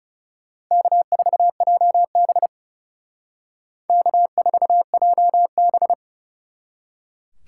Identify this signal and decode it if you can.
Morse code, heard as a keyed tone, about 38 wpm, 700 Hz. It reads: K4JB K4JB